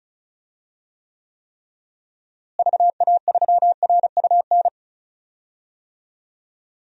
Morse code, heard as a keyed tone, about 34 wpm, 700 Hz. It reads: VA3RUN
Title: DXCC-100 Call Signs - 6 Characters - Repeated 3x - 35wpm